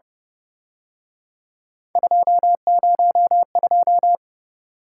Morse code, heard as a keyed tone, about 30 wpm, 700 Hz. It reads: 202
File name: Speed-Racing - Three Letter-Number Combinations - 25wpm